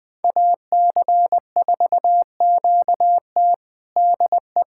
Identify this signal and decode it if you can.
Morse code, heard as a keyed tone, about 20 wpm, 700 Hz. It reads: AC4QT DE
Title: USA Callsign DE Callsign - 5 Characters - 20wpm